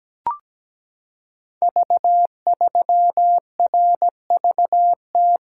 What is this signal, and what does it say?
Morse code, heard as a keyed tone, about 17 wpm, 700 Hz. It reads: V3RVT
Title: Code Groups - 5 Letter:Numbers - 17wpm